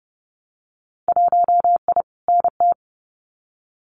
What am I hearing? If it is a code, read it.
Morse code, heard as a keyed tone, about 30 wpm, 700 Hz. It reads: 1I NT
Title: Winter Field Day Exchanges Encore 30wpm